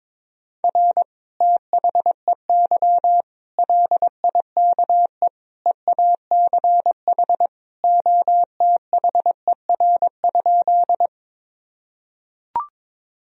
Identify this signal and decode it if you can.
Morse code, heard as a keyed tone, about 21 wpm, 700 Hz. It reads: R THEY LIKE EACH OTHER?
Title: Sentences from Top 200 Words and Abbreviations 22wpm